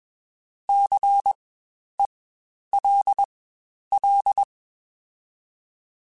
Morse code, heard as a keyed tone, about 21 wpm, spaced as for 10 wpm, 800 Hz. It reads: CELL
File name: Spaced Alphas / Spaced Alphas mp3s / Cs sp @20